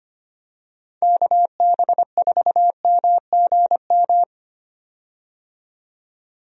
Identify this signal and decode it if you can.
Morse code, heard as a keyed tone, about 25 wpm, 700 Hz. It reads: KB4MGM
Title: USA Call Signs - 6 Characters - 25wpm